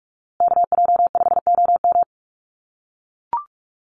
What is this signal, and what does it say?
Morse code, heard as a keyed tone, about 44 wpm, 700 Hz. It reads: KJ5OM
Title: USA Call Signs Repeated 2x 45wpm